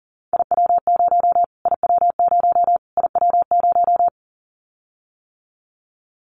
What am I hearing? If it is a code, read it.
Morse code, heard as a keyed tone, about 40 wpm, 700 Hz. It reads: IW0 IW0 IW0
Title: Callsign Prefixes for Top 25 DXCC - Repeated 3x - 40wpm